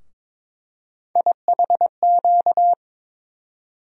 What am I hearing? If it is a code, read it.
Morse code, heard as a keyed tone, about 22 wpm, 700 Hz. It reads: IHQ
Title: USA Call Sign Postfixes - 3 Letters 22wpm